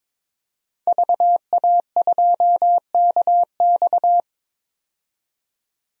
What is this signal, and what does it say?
Morse code, heard as a keyed tone, about 22 wpm, 700 Hz. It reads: VA2KX
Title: DXCC-100 Call Signs - 5 Characters - 22wpm